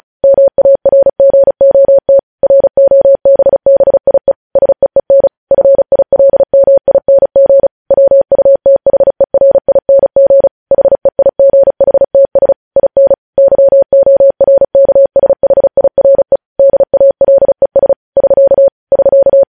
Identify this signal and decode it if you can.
Morse code, heard as a keyed tone, about 36 wpm, 550 Hz. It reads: MARGOT ROBBIE SEEN FILMING WUTHERING HEIGHTS IN YORKSHIRE DALES <SK> <SK>
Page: news headlines in morse code at 35 wpm